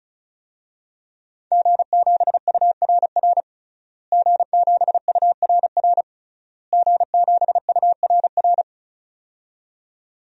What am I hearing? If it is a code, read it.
Morse code, heard as a keyed tone, about 35 wpm, 700 Hz. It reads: G7URR G7URR G7URR
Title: DXCC-100 Call Signs - 5 Characters - Repeated 3x - 35wpm